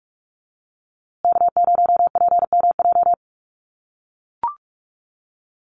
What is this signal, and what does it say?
Morse code, heard as a keyed tone, about 44 wpm, 700 Hz. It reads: K0PMJ
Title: USA Call Signs Repeated 3x 45wpm